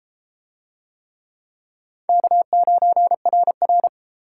Morse code, heard as a keyed tone, about 30 wpm, 700 Hz. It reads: K9RR